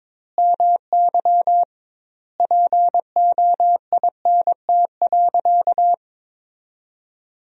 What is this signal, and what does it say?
Morse code, heard as a keyed tone, about 21 wpm, 700 Hz. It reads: MY POINT.
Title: 2x Word-Spacing - Sentences from Top 200 Words - 22wpm